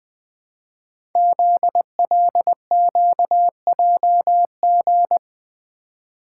Morse code, heard as a keyed tone, about 20 wpm, 700 Hz. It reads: ZLQJG
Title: Code Groups - 5 Letters - 20wpm